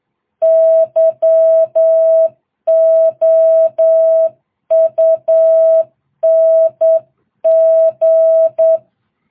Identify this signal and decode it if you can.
Morse code, heard as a keyed tone, about 8 wpm, 660 Hz. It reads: YOUNG